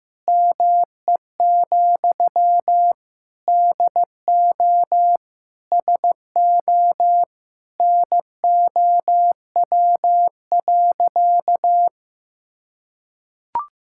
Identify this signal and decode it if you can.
Morse code, heard as a keyed tone, about 15 wpm, 700 Hz. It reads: ME, DO SO NOW.
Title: Sentences from Top 1600 Words 15wpm